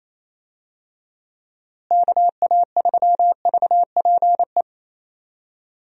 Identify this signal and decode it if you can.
Morse code, heard as a keyed tone, about 28 wpm, 700 Hz. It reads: KA3VPE
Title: USA Call Signs - 6 Characters - Repeated 3x - 28wpm